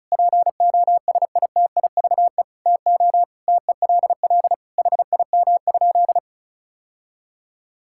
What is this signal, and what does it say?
Morse code, heard as a keyed tone, about 34 wpm, 700 Hz. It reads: POSITIVE TO TELL HIM?
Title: Sentences from Top 1000 Words - Review Part 5 - 35wpm